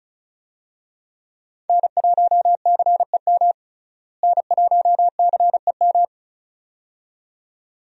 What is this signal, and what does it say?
Morse code, heard as a keyed tone, about 36 wpm, 700 Hz. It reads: N1CEM N1CEM
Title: USA Call Signs - 5 Characters - Repeated 2x - 35wpm